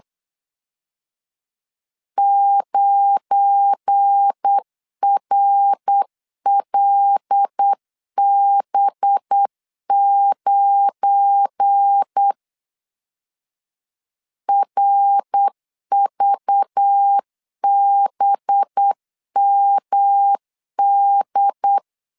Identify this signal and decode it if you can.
Morse code, heard as a keyed tone, about 8 wpm, 800 Hz. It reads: 9RLB9 RVBMD
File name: LEZIONI / 7 GF9Z / 7 GF9 8wpm 3545